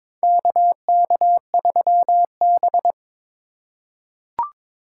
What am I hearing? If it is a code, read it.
Morse code, heard as a keyed tone, about 21 wpm, 700 Hz. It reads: KK3B
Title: USA Call Signs - 4 Characters - 2x Character Spacing - 22wpm